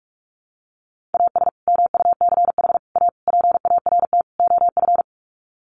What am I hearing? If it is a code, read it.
Morse code, heard as a keyed tone, about 44 wpm, 700 Hz. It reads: AS MUCH A PART OF
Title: Sets of 5 Words Encore2 45wpm